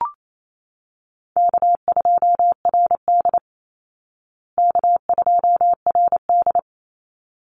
Morse code, heard as a keyed tone, about 28 wpm, 700 Hz. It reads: K2RD K2RD
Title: Call Signs Repeated 3x - 28wpm